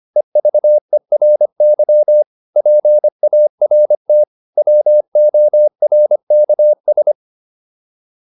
Morse code, heard as a keyed tone, about 25 wpm, 600 Hz. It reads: EVERY PART WORKS